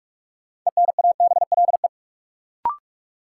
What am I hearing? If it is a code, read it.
Morse code, heard as a keyed tone, about 42 wpm, 700 Hz. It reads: ENABLE
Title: Sentences from Top 1500 Words - New Word Drill - 45wpm